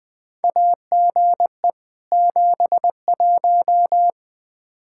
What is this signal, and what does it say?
Morse code, heard as a keyed tone, about 20 wpm, 700 Hz. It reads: AGE 71
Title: Practice QSO - Exchange 3 - More Info - 20wpm